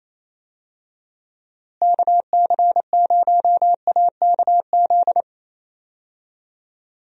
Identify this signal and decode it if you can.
Morse code, heard as a keyed tone, about 28 wpm, 700 Hz. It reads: KC0AKZ
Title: USA Call Signs - 6 Characters - Repeated 2x - 28wpm